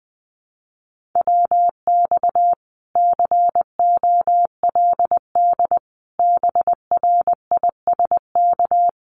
Morse code, heard as a keyed tone, about 20 wpm, 700 Hz. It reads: WX COLD BRISK
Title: Common QSO Phrases Encore 20wpm